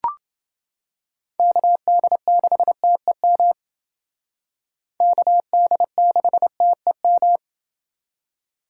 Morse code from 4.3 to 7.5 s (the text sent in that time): KD6TEM